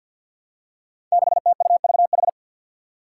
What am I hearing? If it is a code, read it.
Morse code, heard as a keyed tone, about 53 wpm, 700 Hz. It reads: 6TUVH
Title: Five Challenging Letter-Number Combinations 50wpm